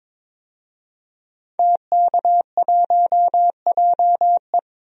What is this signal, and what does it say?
Morse code, heard as a keyed tone, about 22 wpm, 700 Hz. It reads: TK1JE
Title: DXCC-100 Call Signs - 5 Characters - 22wpm